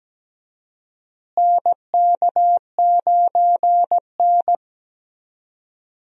Morse code, heard as a keyed tone, about 17 wpm, 700 Hz. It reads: NK9N
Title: USA Call Signs - 4 Characters - Repeated 3x - 17wpm